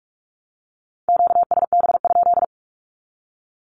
Morse code, heard as a keyed tone, about 44 wpm, 700 Hz. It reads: QSB?
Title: Q-Codes 45wpm